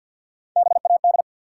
Morse code, heard as a keyed tone, about 52 wpm, 700 Hz. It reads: BAD
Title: Sentences from Top 700 Words - New Word Drill - 50wpm